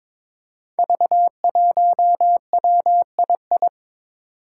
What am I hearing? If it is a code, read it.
Morse code, heard as a keyed tone, about 22 wpm, 700 Hz. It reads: V1WII
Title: Five Challenging Letter-Number Combinations 22wpm